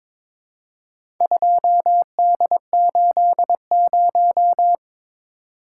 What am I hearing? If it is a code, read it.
Morse code, heard as a keyed tone, about 22 wpm, 700 Hz. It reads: 2D80
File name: practice / 22wpm / Four Letter-Number Combinations - 22wpm